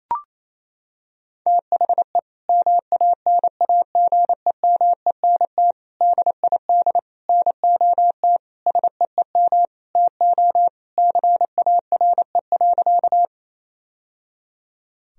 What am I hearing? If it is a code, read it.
Morse code, heard as a keyed tone, about 28 wpm, 700 Hz. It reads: THE MANAGEMENT DID NOT SEEM TO CARE.
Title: Sentences from Top 1000 Words - Review Part 4 - 28wpm